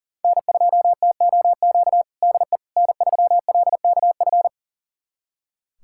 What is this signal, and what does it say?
Morse code, heard as a keyed tone, about 40 wpm, 700 Hz. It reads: N2TOQ DE N3LKF
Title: USA Callsign DE Callsign - 5 Characters - 40wpm